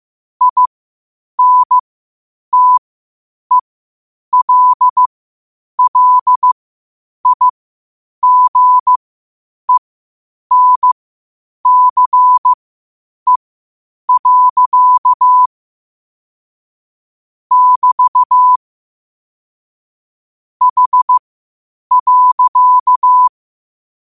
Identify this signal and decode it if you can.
Morse code, heard as a keyed tone, about 15 wpm, spaced as for 8 wpm, 1000 Hz. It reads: INTELLIGENCE. <BT> H.
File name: qotdfiles / 8wpm